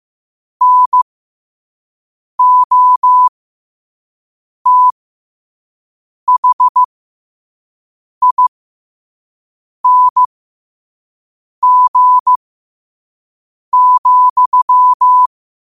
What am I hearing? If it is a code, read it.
Morse code, heard as a keyed tone, about 15 wpm, spaced as for 5 wpm, 1000 Hz. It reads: NOTHING,
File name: qotdfiles / 5wpm